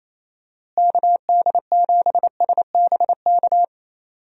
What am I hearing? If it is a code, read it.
Morse code, heard as a keyed tone, about 28 wpm, 700 Hz. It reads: KD7SBK
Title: USA Call Signs - 6 Characters - Repeated 3x - 28wpm